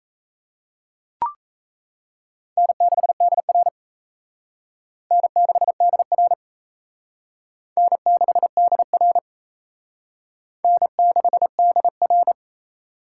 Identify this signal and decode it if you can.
Morse code, heard as a keyed tone, about 34 wpm, 700 Hz. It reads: N6DR N6DR N6DR N6DR
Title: Speed Racing - USA 1x2 Call Signs 28wpm